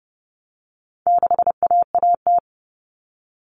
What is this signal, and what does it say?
Morse code, heard as a keyed tone, about 30 wpm, 700 Hz. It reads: 6AAT
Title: Common Cut Numbers - 4-Digit 30wpm